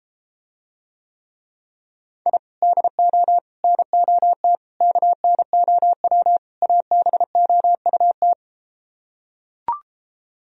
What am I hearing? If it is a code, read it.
Morse code, heard as a keyed tone, about 33 wpm, 700 Hz. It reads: I DO NOT KNOW ABOUT